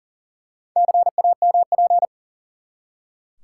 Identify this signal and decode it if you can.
Morse code, heard as a keyed tone, about 38 wpm, 700 Hz. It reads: CAMP
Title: Four Letter Words - 40wpm